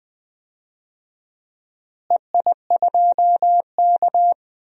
Morse code, heard as a keyed tone, about 19 wpm, 700 Hz. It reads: EI2K
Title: DXCC-100 Call Signs - 4 Characters - Repeated 3x - 20wpm